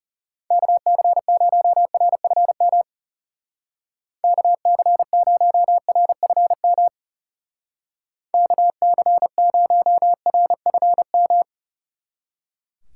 Morse code, heard as a keyed tone, about 34 wpm, 700 Hz. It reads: KC0RFM KC0RFM KC0RFM
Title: Speed Racing - USA 2x3 Call Signs 30wpm